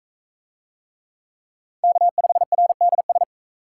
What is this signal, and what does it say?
Morse code, heard as a keyed tone, about 41 wpm, 700 Hz. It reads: K5RDS